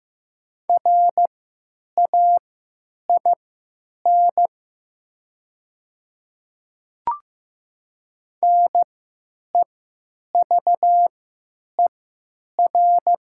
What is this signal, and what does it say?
Morse code, heard as a keyed tone, about 15 wpm, spaced as for 9 wpm, 700 Hz. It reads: RAIN NEVER
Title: 3x Character Spacing - Top 300 Words - 15wpm